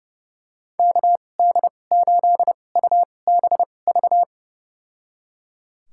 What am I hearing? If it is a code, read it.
Morse code, heard as a keyed tone, about 30 wpm, spaced as for 22 wpm, 700 Hz. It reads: KD8UBV
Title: USA Call Signs - 6 Characters - 2x Character Spacing - 30wpm